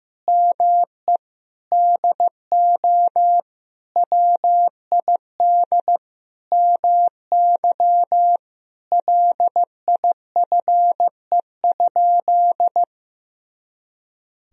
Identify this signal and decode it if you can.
Morse code, heard as a keyed tone, about 15 wpm, 700 Hz. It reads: ME DO WID MY LIFE?
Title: Sentences from Top 300 Words and Abbreviations 15wpm